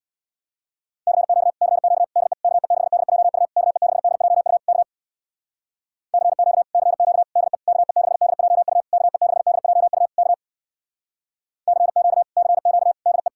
Morse code, heard as a keyed tone, about 64 wpm, 700 Hz. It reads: CQ CQ DE KE6DJU KE6DJU K CQ CQ DE KE6DJU KE6DJU K CQ CQ DE